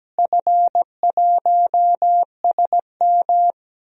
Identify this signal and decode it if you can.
Morse code, heard as a keyed tone, about 17 wpm, 700 Hz. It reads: F1SM
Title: DXCC-100 Call Signs - 4 Characters - Repeated 3x - 17wpm